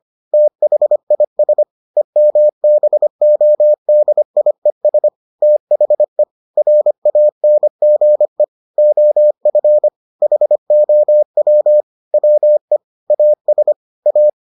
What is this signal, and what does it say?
Morse code, heard as a keyed tone, about 25 wpm, 600 Hz. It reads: THIS EMBODIES THE RANGE OF HOW WE AS A